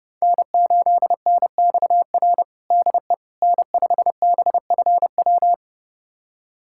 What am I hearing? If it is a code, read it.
Morse code, heard as a keyed tone, about 30 wpm, 700 Hz. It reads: N8NXR DE N5BFW
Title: USA Callsign DE Callsign - 5 Characters - 30wpm